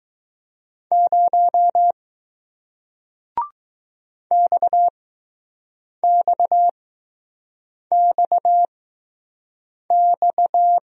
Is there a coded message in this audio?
Yes: Morse code from 0.9 to 10.8 s, heard as a keyed tone, about 20 wpm, spaced as for 6 wpm, 700 Hz. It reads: 0 XXXX